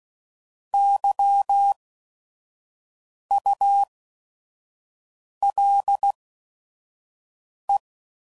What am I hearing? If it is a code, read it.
Morse code, heard as a keyed tone, about 16 wpm, spaced as for 5 wpm, 800 Hz. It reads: YULE